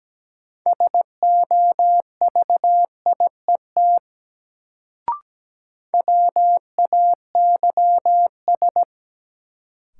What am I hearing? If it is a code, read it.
Morse code, heard as a keyed tone, about 17 wpm, 700 Hz. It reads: SOVIET WAYS
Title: Sets of 5 Words Encore - New Word Drill - 17wpm